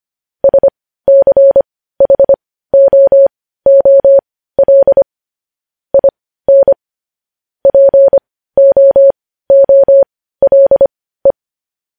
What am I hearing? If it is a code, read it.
Morse code, heard as a keyed tone, about 25 wpm, spaced as for 15 wpm, 550 Hz. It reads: SCHOOL IN POOLE
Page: News Headlines in Morse Code at 15 WPM